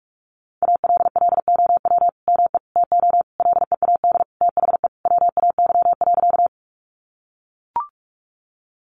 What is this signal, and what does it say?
Morse code, heard as a keyed tone, about 44 wpm, 700 Hz. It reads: ALLOW ME TO LEAD THE WAY.
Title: Sentences from Top 700 Words 45wpm